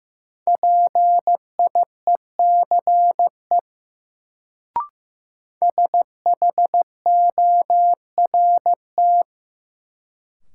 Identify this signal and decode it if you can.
Morse code, heard as a keyed tone, about 15 wpm, 700 Hz. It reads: PIECE SHORT
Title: Sentences from Top 400 Words - New Word Drill - 15wpm